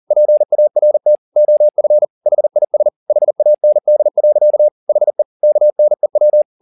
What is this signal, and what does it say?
Morse code, heard as a keyed tone, about 40 wpm, 600 Hz. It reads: PART OF HIS HAND. HE KNEW